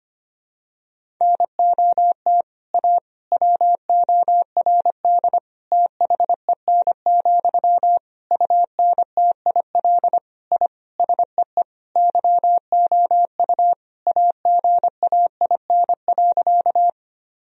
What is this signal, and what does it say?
Morse code, heard as a keyed tone, about 25 wpm, 700 Hz. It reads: NOT A WORD THEN, UNTIL I SEE YOU AGAIN.
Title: Sentences from Top 1000 Words - Review Part 1 - 25wpm